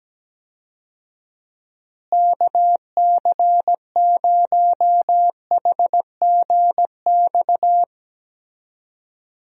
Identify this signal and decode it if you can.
Morse code, heard as a keyed tone, about 17 wpm, 700 Hz. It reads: KC0HGX
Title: USA Call Signs - 6 Characters - Repeated 3x - 17wpm